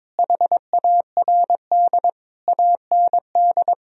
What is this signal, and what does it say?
Morse code, heard as a keyed tone, about 22 wpm, 700 Hz. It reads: HARD AND
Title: Sentences from Top 1000 Words - Review Part 1 - 22wpm